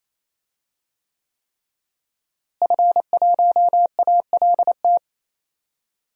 Morse code, heard as a keyed tone, about 28 wpm, 700 Hz. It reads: F1ALT
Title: DXCC-100 Call Signs - 5 Characters - Repeated 3x - 28wpm